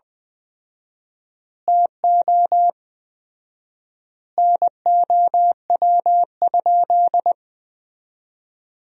Morse code, heard as a keyed tone, about 20 wpm, 700 Hz. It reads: TO NOW?
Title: 4x Word-Spacing - Sentences from Top 100 Words - 20wpm